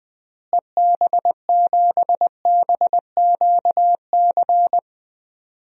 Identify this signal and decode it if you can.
Morse code, heard as a keyed tone, about 20 wpm, 700 Hz. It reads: EB7BQC
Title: DXCC-100 Call Signs - 6 Characters - Repeated 2x - 20wpm